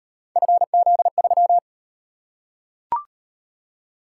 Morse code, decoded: FZ3

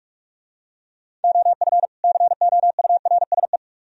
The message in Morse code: OF COURSE

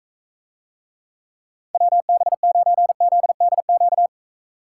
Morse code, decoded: WB9ZDQ

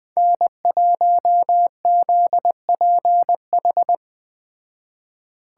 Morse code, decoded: N1ZPH

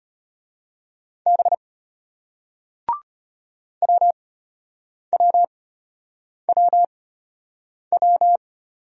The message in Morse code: B WWWW

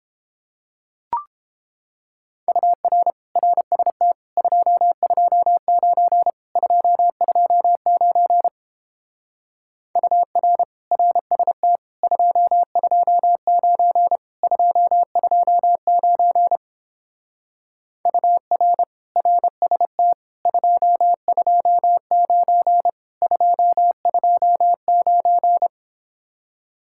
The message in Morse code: UR RST 229 229 UR RST 229 229 UR RST 229 229

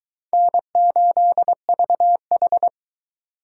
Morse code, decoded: N8VH

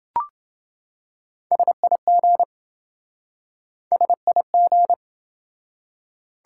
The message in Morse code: SIG SIG

